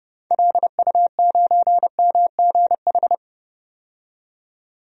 Morse code, heard as a keyed tone, about 30 wpm, 700 Hz. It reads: LU9MGH
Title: DXCC-100 Call Signs - 6 Characters - 30wpm